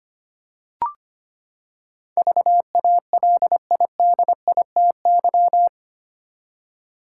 Morse code, heard as a keyed tone, about 25 wpm, 700 Hz. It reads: VALIDITY
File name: Eight Letter Words - 25wpm